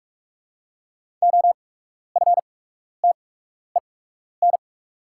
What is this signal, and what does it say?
Morse code, heard as a keyed tone, about 44 wpm, spaced as for 12 wpm, 690 Hz. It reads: OFTEN
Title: 8x Character Spacing - Top 500 Words - 45wpm